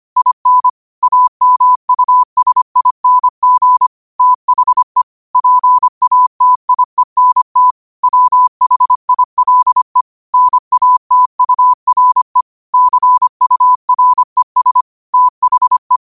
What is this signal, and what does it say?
Morse code, heard as a keyed tone, about 25 wpm, 1000 Hz. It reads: IN AMUSING THE PATIENT WHILE NATURE CURES THE
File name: qotdfiles / 25wpm